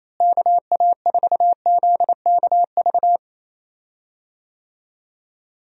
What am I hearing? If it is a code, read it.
Morse code, heard as a keyed tone, about 28 wpm, 700 Hz. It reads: KA4ZKV
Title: USA Call Signs - 6 Characters - Repeated 3x - 28wpm